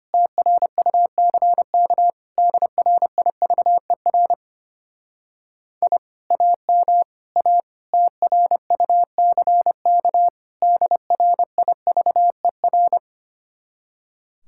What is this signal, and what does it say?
Morse code, heard as a keyed tone, about 31 wpm, 700 Hz. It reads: TRUCK DRIVER I AM A TRUCK DRIVER
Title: Speed-Racing Common QSO Phrases Encore - 25wpm